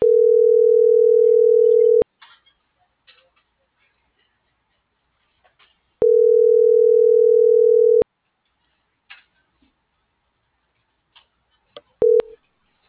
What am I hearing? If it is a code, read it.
Ringback tone (North America), 440 + 480 Hz.